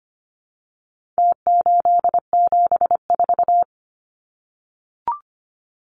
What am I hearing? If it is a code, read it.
Morse code, heard as a keyed tone, about 25 wpm, 700 Hz. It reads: T874
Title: Numbers - Contest Style - 4-Digit 25wpm